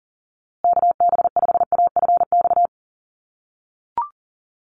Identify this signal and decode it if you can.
Morse code, heard as a keyed tone, about 40 wpm, 700 Hz. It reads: KB5AFX
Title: USA Call Signs Repeated 3x 40wpm